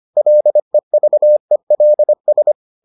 Morse code, heard as a keyed tone, about 25 wpm, 600 Hz. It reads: LEVELS